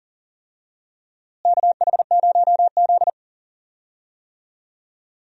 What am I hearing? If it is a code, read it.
Morse code, heard as a keyed tone, about 40 wpm, 700 Hz. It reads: KH0Z
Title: USA Call Signs - 4 Characters - Repeated 3x - 40wpm